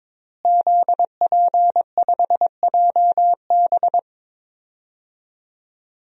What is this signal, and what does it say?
Morse code, heard as a keyed tone, about 22 wpm, 700 Hz. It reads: ZP5JB
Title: DXCC-100 Call Signs - 5 Characters - Repeated 2x - 22wpm